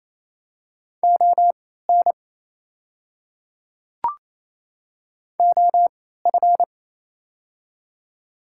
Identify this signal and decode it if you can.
Morse code, heard as a keyed tone, about 28 wpm, spaced as for 16 wpm, 700 Hz. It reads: ON OF
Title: 3x Character Spacing - Top 200 Words - 28wpm